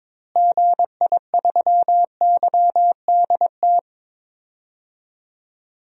Morse code, heard as a keyed tone, about 22 wpm, 700 Hz. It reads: GI3YDT
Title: DXCC-100 Call Signs - 6 Characters - Repeated 2x - 22wpm